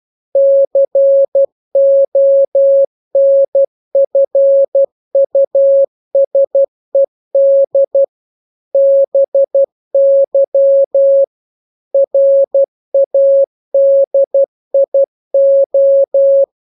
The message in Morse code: CONFUSED BY RADIO